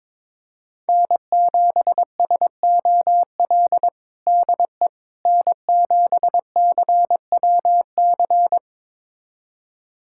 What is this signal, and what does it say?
Morse code, heard as a keyed tone, about 22 wpm, 700 Hz. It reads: N7SOL DE N7CWC
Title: USA Callsign DE Callsign - 5 Characters - 22wpm